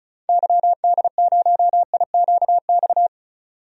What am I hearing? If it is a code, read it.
Morse code, heard as a keyed tone, about 34 wpm, 700 Hz. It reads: YD0IQX
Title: DXCC-100 Call Signs - 6 Characters - 35wpm